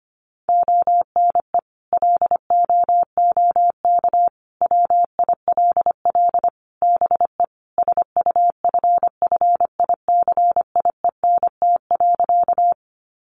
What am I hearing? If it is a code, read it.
Morse code, heard as a keyed tone, about 25 wpm, 700 Hz. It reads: ONE LOOK WILL BE SUFFICIENT.
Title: Sentences from Top 1400 Words 25wpm